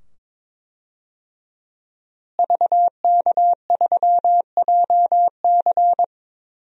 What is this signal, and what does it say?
Morse code, heard as a keyed tone, about 22 wpm, 700 Hz. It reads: VK3JC
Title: DXCC-100 Call Signs - 5 Characters - Repeated 2x - 22wpm